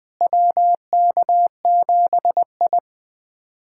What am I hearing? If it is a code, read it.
Morse code, heard as a keyed tone, about 21 wpm, 700 Hz. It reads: WK7I
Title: USA Callsign DE Callsign - 4 Characters - 20wpm